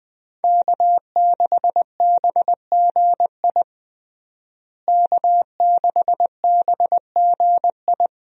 Morse code, heard as a keyed tone, about 20 wpm, 700 Hz. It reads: K6BGI K6BGI
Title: USA Call Signs - 5 Characters - Repeated 2x - 20wpm